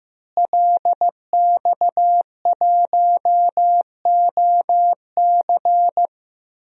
Morse code, heard as a keyed tone, about 14 wpm, 700 Hz. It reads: LX1OC
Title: DXCC-100 Call Signs - 5 Characters - Repeated 2x - 15wpm